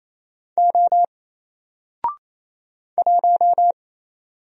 Morse code, heard as a keyed tone, about 27 wpm, 700 Hz. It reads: O 1
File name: Single-Letter-Number-28wpm